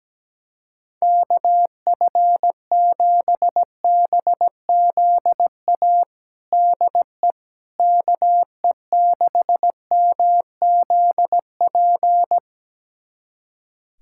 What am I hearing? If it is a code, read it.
Morse code, heard as a keyed tone, about 17 wpm, 700 Hz. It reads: KF7BZA DE KE6MZP
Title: USA Callsign DE Callsign - 6 Characters - 17wpm